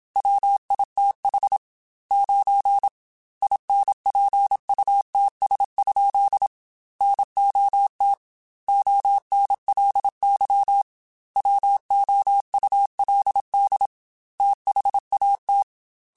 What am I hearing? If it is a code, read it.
Morse code, heard as a keyed tone, about 27 wpm, 800 Hz. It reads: WITH 9 INPUTS? NOT ONLY WOULD THAT
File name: Technical Subjects / Technical Subjects-mp3s / AND Gate @25